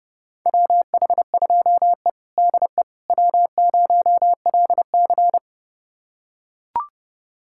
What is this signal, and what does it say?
Morse code, heard as a keyed tone, about 32 wpm, 700 Hz. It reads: WH2E DE W0LC
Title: USA Callsign DE Callsign - 4 Characters - 30wpm